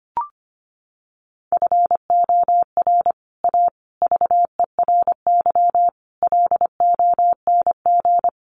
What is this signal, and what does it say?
Morse code, heard as a keyed tone, about 25 wpm, 700 Hz. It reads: FOR A VERY LONG
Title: Sets of 5 Words 25wpm